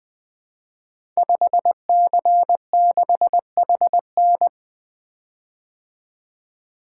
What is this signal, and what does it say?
Morse code, heard as a keyed tone, about 20 wpm, 700 Hz. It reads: 5C6HN
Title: Code Groups - 5 Letter:Numbers - 20wpm